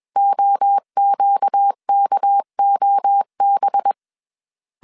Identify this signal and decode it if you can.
Morse code, heard as a keyed tone, about 21 wpm, 800 Hz. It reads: OQKOB
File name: CW / DET / 1 6 1500C S A 20wpm 2048